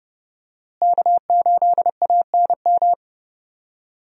Morse code, heard as a keyed tone, about 30 wpm, 700 Hz. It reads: K8ANM
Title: USA Call Signs - 5 Characters - Repeated 3x - 30wpm